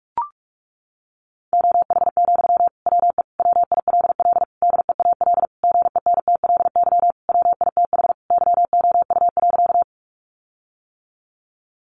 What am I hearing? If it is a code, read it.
Morse code, heard as a keyed tone, about 44 wpm, 700 Hz. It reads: OH, WE WILL DEAL GENTLY WITH YOU.
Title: Sentences from Top 1500 Words 45wpm